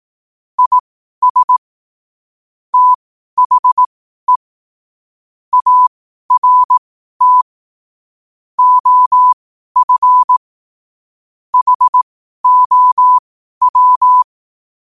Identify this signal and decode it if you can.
Morse code, heard as a keyed tone, about 18 wpm, spaced as for 12 wpm, 1000 Hz. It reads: IS THE ART OF HOW
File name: qotdfiles / 12wpm